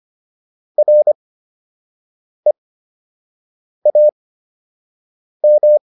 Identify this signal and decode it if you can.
Morse code, heard as a keyed tone, about 25 wpm, spaced as for 6 wpm, 600 Hz. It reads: REAM